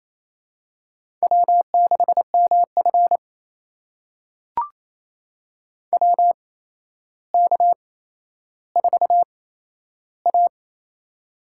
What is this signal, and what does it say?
Morse code, heard as a keyed tone, about 28 wpm, spaced as for 12 wpm, 700 Hz. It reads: W6MF WK4A
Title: USA Call Signs - 4 Characters - 8x Character Spacing - 28wpm